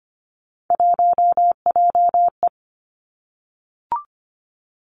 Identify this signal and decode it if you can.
Morse code, heard as a keyed tone, about 25 wpm, 700 Hz. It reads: 1JE
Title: Three Challenging Letter-Number Combinations 25wpm